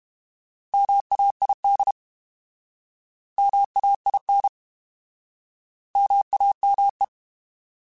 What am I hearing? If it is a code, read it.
Morse code, heard as a keyed tone, about 32 wpm, 800 Hz. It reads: MAID MAIN MAME
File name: Alphabetical / Alphabetical mp3s / Ms @30